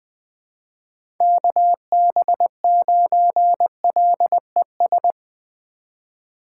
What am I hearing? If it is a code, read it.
Morse code, heard as a keyed tone, about 20 wpm, 700 Hz. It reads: KB9LES
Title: USA Call Signs - 6 Characters - 4x Character Spacing - 20wpm